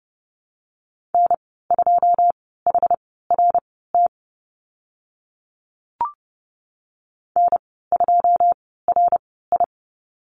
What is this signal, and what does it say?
Morse code, heard as a keyed tone, about 30 wpm, spaced as for 17 wpm, 690 Hz. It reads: N2HRT N2RI
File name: Lesson 15g - Number 5 - Call Signs 30wpm